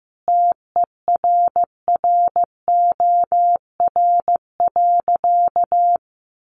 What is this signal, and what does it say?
Morse code, heard as a keyed tone, about 15 wpm, 700 Hz. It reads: TERROR.
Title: A Sound Of Thunder 15wpm